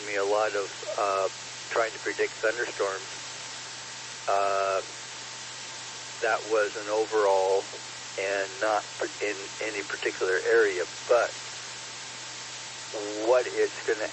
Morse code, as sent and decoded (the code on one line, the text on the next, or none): none